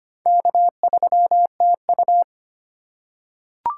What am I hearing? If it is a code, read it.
Morse code, heard as a keyed tone, about 25 wpm, 700 Hz. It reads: K3TU
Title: USA Call Signs - 4 Characters - 2x Character Spacing - 25wpm